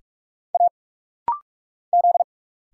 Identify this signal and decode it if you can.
Morse code, heard as a keyed tone, about 46 wpm, 700 Hz. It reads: A Z
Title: Single Letters - Rapid-Fire 45wpm